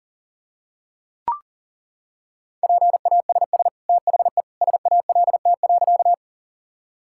Morse code, heard as a keyed tone, about 40 wpm, 700 Hz. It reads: PASS THE SALT.
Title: Sentences from Top 1000 Words - Review Part 5 - 40wpm